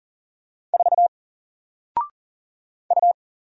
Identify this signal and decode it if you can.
Morse code, heard as a keyed tone, about 40 wpm, 700 Hz. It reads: 4 U
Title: Single Character (Letter:Number:Punctuation:Prosign) 40wpm